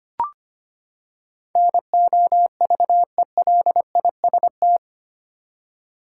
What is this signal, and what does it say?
Morse code, heard as a keyed tone, about 25 wpm, 700 Hz. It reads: NOVELIST
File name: Eight Letter Words - 25wpm